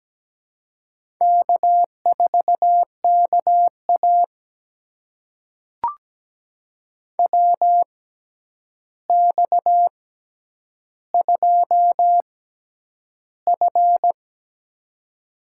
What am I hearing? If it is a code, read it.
Morse code, heard as a keyed tone, about 17 wpm, spaced as for 9 wpm, 700 Hz. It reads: K4KA WX2F
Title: USA Call Signs - 4 Characters - 6x Character Spacing - 17wpm